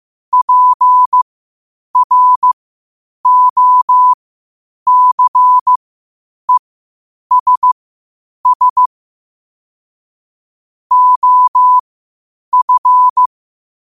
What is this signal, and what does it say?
Morse code, heard as a keyed tone, about 15 wpm, spaced as for 8 wpm, 1000 Hz. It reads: PROCESS OF
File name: qotdfiles / 8wpm